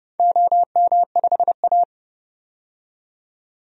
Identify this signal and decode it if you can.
Morse code, heard as a keyed tone, about 30 wpm, 700 Hz. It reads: OM5A
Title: DXCC-100 Call Signs - 4 Characters - 30wpm